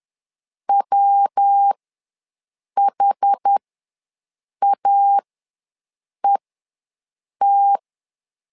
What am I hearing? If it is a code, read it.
Morse code, heard as a keyed tone, about 10 wpm, spaced as for 6 wpm, 800 Hz. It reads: WHAET
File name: LEZIONI / 3 MWRO / 3 MW ws 10wpm 2213